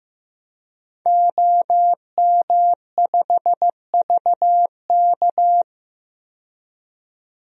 Morse code, heard as a keyed tone, about 15 wpm, 700 Hz. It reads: OM5VK
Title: DXCC-100 Call Signs - 5 Characters - 15wpm